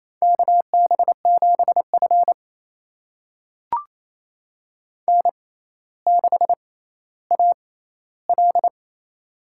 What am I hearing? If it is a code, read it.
Morse code, heard as a keyed tone, about 28 wpm, spaced as for 14 wpm, 700 Hz. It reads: KB7F N6AL
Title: USA Call Signs - 4 Characters - 6x Character Spacing - 28wpm